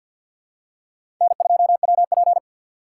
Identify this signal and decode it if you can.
Morse code, heard as a keyed tone, about 52 wpm, 700 Hz. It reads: N2WP
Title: USA Call Signs - 6x Character Spacing - 50wpm